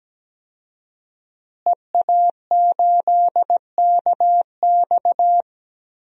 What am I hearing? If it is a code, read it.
Morse code, heard as a keyed tone, about 17 wpm, 700 Hz. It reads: EA8KX
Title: DXCC-100 Call Signs - 5 Characters - Repeated 3x - 17wpm